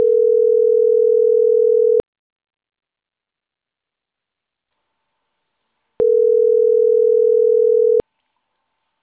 Ringback tone (North America), 440 + 480 Hz.